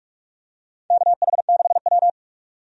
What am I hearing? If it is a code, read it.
Morse code, heard as a keyed tone, about 44 wpm, 700 Hz. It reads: KH6W
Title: DXCC-100 Call Signs - 4 Characters - 45wpm